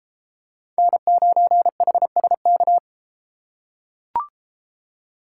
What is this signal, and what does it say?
Morse code, heard as a keyed tone, about 32 wpm, 700 Hz. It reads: N9HSK